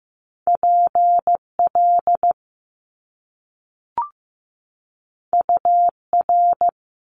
Morse code, heard as a keyed tone, about 15 wpm, 700 Hz. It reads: PL UR
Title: Top Two Letter Combinations in Words 15wpm